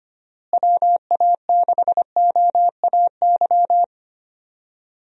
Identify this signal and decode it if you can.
Morse code, heard as a keyed tone, about 25 wpm, 700 Hz. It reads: WA6OAY
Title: USA Call Signs - 6 Characters - Repeated 2x - 25wpm